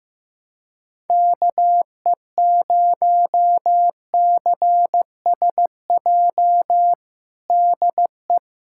Morse code, heard as a keyed tone, about 15 wpm, 700 Hz. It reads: KE0CSJ DE
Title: USA Callsign DE Callsign - 6 Characters - 15wpm